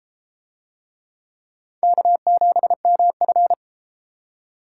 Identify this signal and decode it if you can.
Morse code, heard as a keyed tone, about 33 wpm, 700 Hz. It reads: K7MF